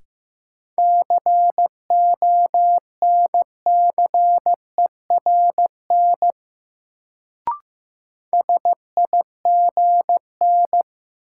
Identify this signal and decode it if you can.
Morse code, heard as a keyed tone, about 15 wpm, 700 Hz. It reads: CONCERN SIGN
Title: Sentences from Top 1000 Words - New Word Drill - 15wpm